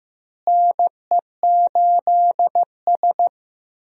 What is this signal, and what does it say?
Morse code, heard as a keyed tone, about 15 wpm, 700 Hz. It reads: NE8S
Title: USA Call Signs - 4 Characters - Repeated 3x - 15wpm